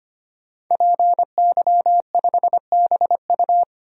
Morse code, heard as a keyed tone, about 25 wpm, 700 Hz. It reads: PY5BU
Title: DXCC-100 Call Signs - 5 Characters - 25wpm